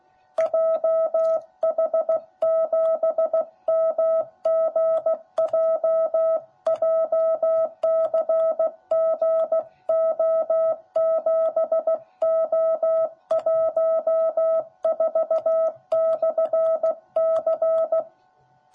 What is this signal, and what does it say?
Morse code, heard as a keyed tone, about 15 wpm, 660 Hz. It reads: JH7MGJ JCGO7O14/C
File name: jh8cbh / 2021cw / cw10b